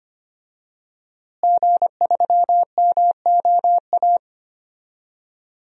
Morse code, heard as a keyed tone, about 25 wpm, 700 Hz. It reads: G3MOA
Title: DXCC-100 Call Signs - 5 Characters - Repeated 3x - 25wpm